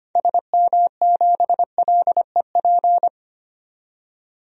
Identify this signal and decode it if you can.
Morse code, heard as a keyed tone, about 25 wpm, 690 Hz. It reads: SM7LEP
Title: DXCC-100 Call Signs - 6 Characters - 25wpm